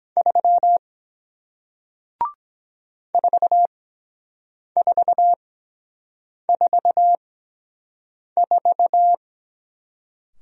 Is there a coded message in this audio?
Yes: Morse code from 0.2 to 9.1 s, heard as a keyed tone, about 23 wpm, spaced as for 7 wpm, 700 Hz. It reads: 3 4444